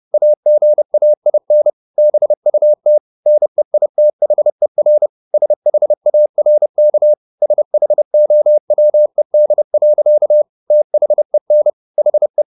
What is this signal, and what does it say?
Morse code, heard as a keyed tone, about 30 wpm, 600 Hz. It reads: AGAIN BUT NEITHER SHARK SHOWED. THEN HE